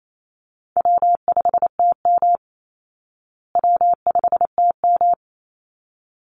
Morse code, heard as a keyed tone, about 28 wpm, 700 Hz. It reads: W5TM W5TM
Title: Call Signs Repeated 3x - 28wpm